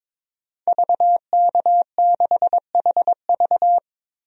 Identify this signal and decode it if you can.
Morse code, heard as a keyed tone, about 22 wpm, 700 Hz. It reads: VK6HV
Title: DXCC-100 Call Signs - 5 Characters - 22wpm